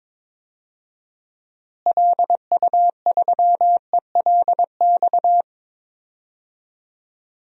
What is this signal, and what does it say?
Morse code, heard as a keyed tone, about 22 wpm, 700 Hz. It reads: LU3ELX
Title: DXCC-100 Call Signs - 6 Characters - Repeated 3x - 22wpm